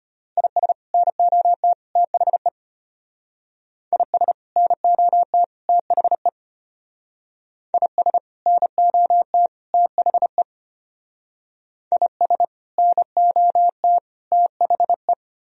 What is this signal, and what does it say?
Morse code, heard as a keyed tone, about 31 wpm, 700 Hz. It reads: IS NOT THE IS NOT THE IS NOT THE IS NOT THE
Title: Speed-Racing - Sets of 3 Words 25wpm